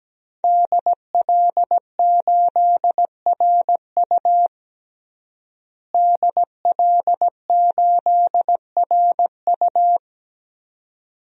Morse code, heard as a keyed tone, about 17 wpm, 700 Hz. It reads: DL8RU DL8RU
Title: DXCC-100 Call Signs - 5 Characters - Repeated 2x - 17wpm